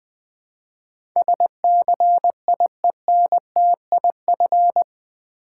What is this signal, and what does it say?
Morse code, heard as a keyed tone, about 20 wpm, 700 Hz. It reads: SCIENTIF
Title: Build A Word - 4 Letters At A Time - 10-Letter Words - 20wpm